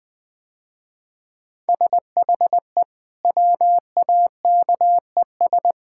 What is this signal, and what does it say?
Morse code, heard as a keyed tone, about 21 wpm, 700 Hz. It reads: SHE WAKES
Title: Easy Sentences with Hints - 20wpm